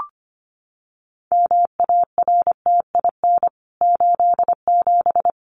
Morse code, heard as a keyed tone, about 25 wpm, 700 Hz. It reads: MARTIN 87
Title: CWT Exchange 25wpm